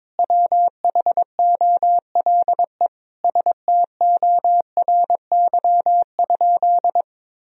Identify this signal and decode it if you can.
Morse code, heard as a keyed tone, about 21 wpm, 700 Hz. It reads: WHOLE STORY?
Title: Sentences from Top 400 Words and Abbreviations 22wpm